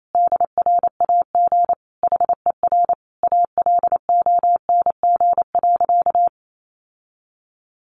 Morse code, heard as a keyed tone, about 28 wpm, 700 Hz. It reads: DRAG HER ALONG.
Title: Sentences from Top 1500 Words 28wpm